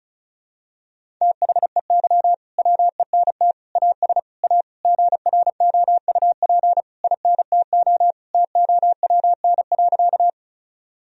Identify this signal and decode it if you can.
Morse code, heard as a keyed tone, about 34 wpm, 700 Hz. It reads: THEY WENT AS A GROUP INTO TOWN.
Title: Sentences from Top 1000 Words - Review Part 1 - 35wpm